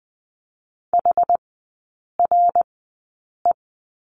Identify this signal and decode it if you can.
Morse code, heard as a keyed tone, about 20 wpm, spaced as for 8 wpm, 700 Hz. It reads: HRE
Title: 060 Lesson 11e - Letter h - Three Characters 20wpm